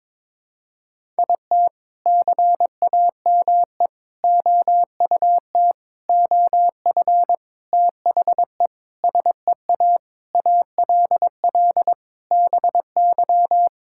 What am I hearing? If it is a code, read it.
Morse code, heard as a keyed tone, about 22 wpm, 700 Hz. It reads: IT CAME OUT OF THE SEA ALL BY
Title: Sentences from Top 1000 Words - Review Part 5 - 22wpm